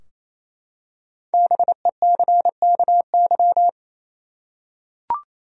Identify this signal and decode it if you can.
Morse code, heard as a keyed tone, about 27 wpm, 700 Hz. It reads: BECKY
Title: USA Common Names - Max 5 Letters - 28wpm